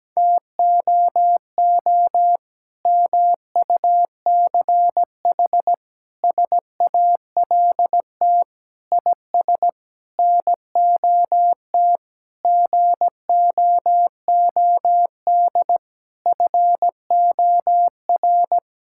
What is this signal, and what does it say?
Morse code, heard as a keyed tone, about 17 wpm, 700 Hz. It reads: TOO MUCH SALT IS NOT GOOD FOR
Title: Sentences from Top 1000 Words - Review Part 5 - 17wpm